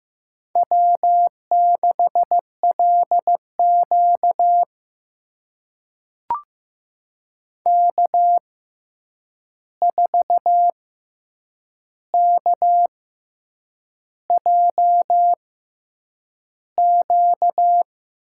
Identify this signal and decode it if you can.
Morse code, heard as a keyed tone, about 15 wpm, spaced as for 7 wpm, 700 Hz. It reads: W6LQ K4KJQ
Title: USA Call Signs - 6x Character Spacing - 15wpm